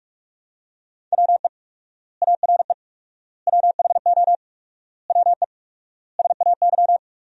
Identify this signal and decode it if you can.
Morse code, heard as a keyed tone, about 43 wpm, 700 Hz. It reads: WE ARE WHO WE SAY
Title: 4x Word-Spacing - Sentences from Top 200 Words - 45wpm